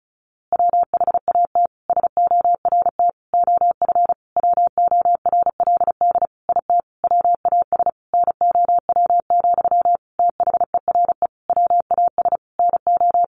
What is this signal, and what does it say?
Morse code, heard as a keyed tone, about 34 wpm, 700 Hz. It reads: WHAT SORT OF WORLD IT WAS NOW, THERE WAS NO
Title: A Sound Of Thunder 35wpm